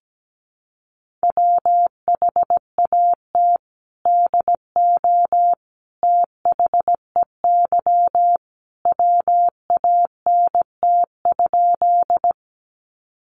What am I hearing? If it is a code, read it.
Morse code, heard as a keyed tone, about 17 wpm, 700 Hz. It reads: WHAT DO THEY WANT?
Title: Sentences from Top 200 Words 17wpm